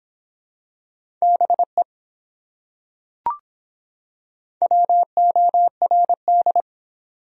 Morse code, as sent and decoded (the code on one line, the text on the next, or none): -... / .-- --- .-. -..
BE WORD